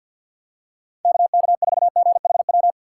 Morse code, heard as a keyed tone, about 52 wpm, 700 Hz. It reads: KK4GHW
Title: USA Call Signs - 6 Characters - 4x Character Spacing - 50wpm